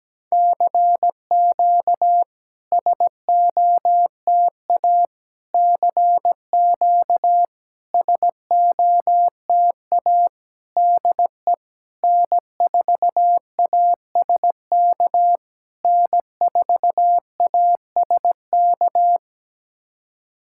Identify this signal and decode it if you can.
Morse code, heard as a keyed tone, about 17 wpm, 700 Hz. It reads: CQ SOTA CQ SOTA DE N4ASK N4ASK
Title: SOTA QSO Practice 17wpm